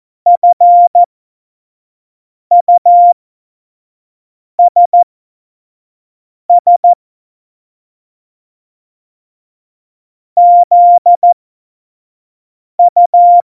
Morse code, heard as a keyed tone, about 14 wpm, spaced as for 5 wpm, 700 Hz. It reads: FUSS ZU
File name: Morsekurs / Witze / Witz 22. August 17, Tempo 25 F70